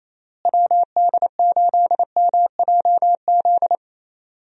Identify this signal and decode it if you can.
Morse code, heard as a keyed tone, about 28 wpm, 700 Hz. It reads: WD8MJZ
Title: USA Call Signs - 6 Characters - Repeated 3x - 28wpm